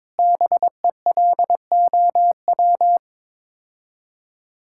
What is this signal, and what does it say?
Morse code, heard as a keyed tone, about 22 wpm, 700 Hz. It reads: BELOW
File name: Sets of 9 Words - 22wpm